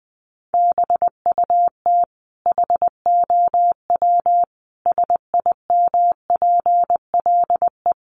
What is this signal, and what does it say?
Morse code, heard as a keyed tone, about 20 wpm, 700 Hz. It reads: BUT HOW SIMPLE
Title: Sentences from Top 400 Words 20wpm